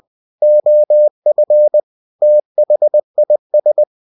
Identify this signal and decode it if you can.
Morse code, heard as a keyed tone, about 20 wpm, 600 Hz. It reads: OF THIS
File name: cw-books / winnie-the-pooh / winnie-20wpm-page-0114